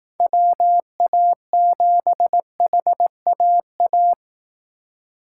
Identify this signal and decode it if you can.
Morse code, heard as a keyed tone, about 18 wpm, 700 Hz. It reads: WA7HAA
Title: Speed Racing - USA 2x3 Call Signs 15wpm